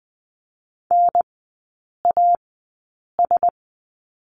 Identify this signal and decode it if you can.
Morse code, heard as a keyed tone, about 20 wpm, spaced as for 8 wpm, 700 Hz. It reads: NAS
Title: 047 Lesson 9e - Number 4 - Three Characters 20wpm